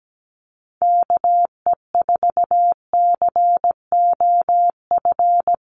Morse code, heard as a keyed tone, about 17 wpm, 700 Hz. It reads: KE4COF
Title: USA Call Signs Repeated 3x 17wpm